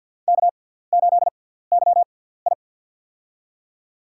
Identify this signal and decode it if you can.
Morse code, heard as a keyed tone, about 47 wpm, 700 Hz. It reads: K 8 Y I
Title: USA Call Signs - 6x Character Spacing - 50wpm